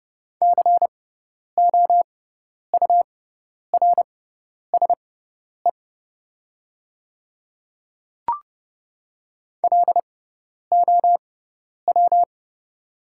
Morse code, heard as a keyed tone, about 30 wpm, spaced as for 10 wpm, 700 Hz. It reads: COURSE LOW
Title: 6x Character Spacing - Top 500 Words - 30wpm